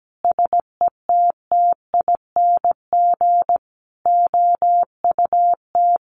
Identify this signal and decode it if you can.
Morse code, heard as a keyed tone, about 17 wpm, 700 Hz. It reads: SETTING OUT
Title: Sentences from Top 1300 Words 17wpm